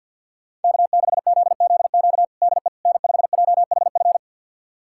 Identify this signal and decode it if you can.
Morse code, heard as a keyed tone, about 51 wpm, 700 Hz. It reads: K6ZZQ DE N5JHF
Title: USA Callsign DE Callsign - 5 Characters - 50wpm